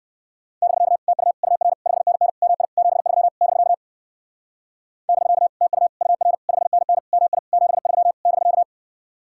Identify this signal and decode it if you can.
Morse code, heard as a keyed tone, about 56 wpm, 700 Hz. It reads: <BK> TU UR 5NN MI 73 <BK> <BK> TU UR 5NN MI 73 <BK>
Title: Speed-Racing Common QSO Phrases Encore - 45wpm